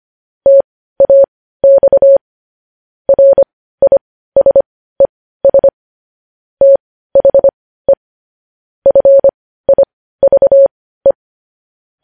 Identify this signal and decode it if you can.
Morse code, heard as a keyed tone, about 25 wpm, spaced as for 15 wpm, 550 Hz. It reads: TAX RISES THE FIVE
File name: podcast / News headlines-1x-25x15